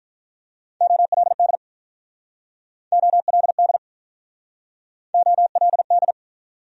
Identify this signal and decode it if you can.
Morse code, heard as a keyed tone, about 46 wpm, 700 Hz. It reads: OLD OLD OLD